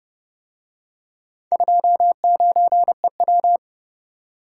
Morse code, heard as a keyed tone, about 30 wpm, 700 Hz. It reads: 29EW